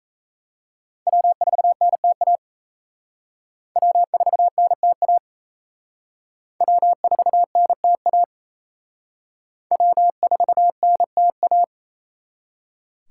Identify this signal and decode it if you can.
Morse code, heard as a keyed tone, about 35 wpm, 700 Hz. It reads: W4NTA W4NTA W4NTA W4NTA
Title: Speed Racing - USA Call Signs 28wpm